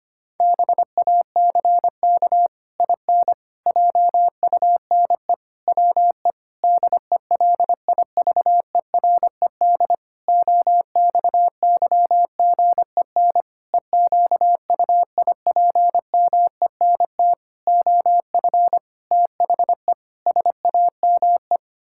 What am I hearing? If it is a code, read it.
Morse code, heard as a keyed tone, about 25 wpm, 700 Hz. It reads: BACK IN JUNE WE DELIVERED OXYGEN EQUIPMENT OF THE SAME
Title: Pangram Challenge v2 - 25wpm